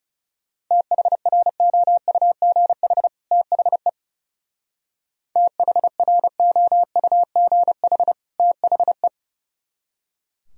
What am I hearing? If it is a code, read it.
Morse code, heard as a keyed tone, about 37 wpm, 700 Hz. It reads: THROUGH THE THROUGH THE
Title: Speed-Racing - Sets of 2 Words 30wpm